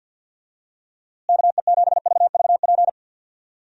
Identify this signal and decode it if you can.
Morse code, heard as a keyed tone, about 51 wpm, 700 Hz. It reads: KE7VVP
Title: USA Call Signs - 6 Characters - Repeated 3x - 50wpm